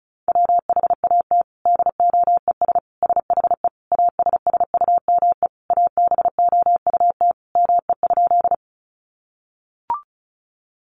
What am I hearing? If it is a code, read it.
Morse code, heard as a keyed tone, about 34 wpm, 700 Hz. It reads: WHAT DOES SHE ASSUME ABOUT ME?
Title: Sentences from Top 1500 Words 35wpm